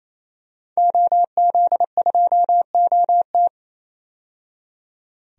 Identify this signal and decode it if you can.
Morse code, heard as a keyed tone, about 27 wpm, 700 Hz. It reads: OZ2OT
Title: DXCC-100 Call Signs - 5 Characters - 28wpm